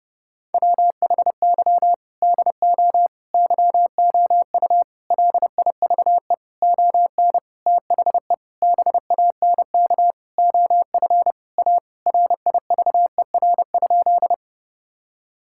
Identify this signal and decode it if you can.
Morse code, heard as a keyed tone, about 32 wpm, 700 Hz. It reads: WHY DO YOU LIVE ON THE BANK OF A RIVER?
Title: Sentences from Top 1000 Words - Review Part 6 - 30wpm